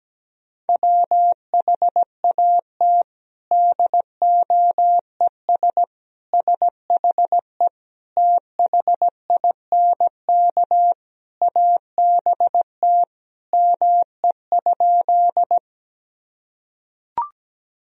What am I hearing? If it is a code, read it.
Morse code, heard as a keyed tone, about 17 wpm, 700 Hz. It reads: WHAT DOES SHE THINK ABT ME?
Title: Sentences from Top 200 Words and Abbreviations 17wpm